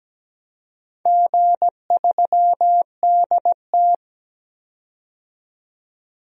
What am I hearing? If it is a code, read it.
Morse code, heard as a keyed tone, about 17 wpm, 700 Hz. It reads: G3DT
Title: DXCC-100 Call Signs - 4 Characters - 17wpm